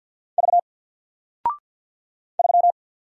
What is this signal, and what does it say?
Morse code, heard as a keyed tone, about 50 wpm, 700 Hz. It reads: V 3
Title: Single Character (Letter:Number:Punctuation:Prosign) 50wpm